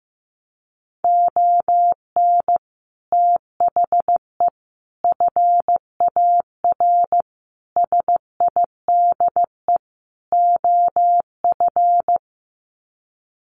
Morse code, heard as a keyed tone, about 15 wpm, 700 Hz. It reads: ON THE FAR SIDE OF
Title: Sets of 5 Words 15wpm